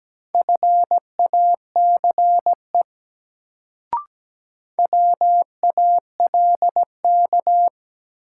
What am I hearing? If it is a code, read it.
Morse code, heard as a keyed tone, about 17 wpm, 700 Hz. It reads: FACE WALK
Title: Sentences from Top 300 Words - New Word Drill - 17wpm